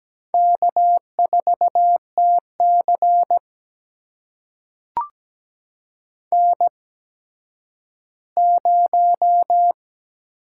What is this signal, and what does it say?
Morse code, heard as a keyed tone, about 17 wpm, spaced as for 10 wpm, 700 Hz. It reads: K4TC N0
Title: USA Call Signs - 4 Characters - 8x Character Spacing - 17wpm